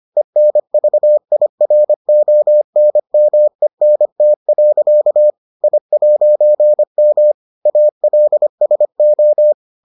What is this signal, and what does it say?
Morse code, heard as a keyed tone, about 25 wpm, 600 Hz. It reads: ENVIRONMENT. I'M ALSO